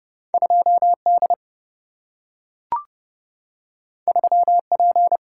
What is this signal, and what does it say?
Morse code, heard as a keyed tone, about 30 wpm, 700 Hz. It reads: 2D 3P
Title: Single-Letter Single-Number 30wpm